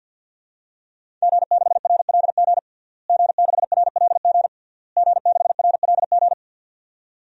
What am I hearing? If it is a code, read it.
Morse code, heard as a keyed tone, about 51 wpm, 700 Hz. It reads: G6RLG G6RLG G6RLG
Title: DXCC-100 Call Signs - 5 Characters - Repeated 3x - 50wpm